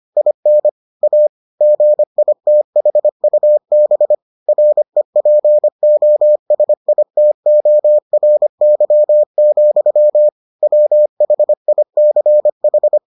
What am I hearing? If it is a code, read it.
Morse code, heard as a keyed tone, about 25 wpm, 600 Hz. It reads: IN A GITHUB REPOSITORY, WHICH